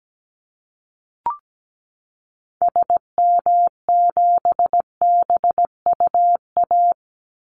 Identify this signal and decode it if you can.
Morse code, heard as a keyed tone, about 17 wpm, 700 Hz. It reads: SM7BUA
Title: Call Signs Repeated 2x - 17wpm